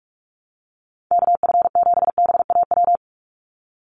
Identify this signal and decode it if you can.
Morse code, heard as a keyed tone, about 46 wpm, 700 Hz. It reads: KF7BAW
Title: USA Call Signs 45wpm